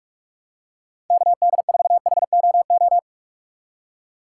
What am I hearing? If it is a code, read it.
Morse code, heard as a keyed tone, about 45 wpm, 700 Hz. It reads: KD4HOO
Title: USA Call Signs - 6 Characters - 6x Character Spacing - 45wpm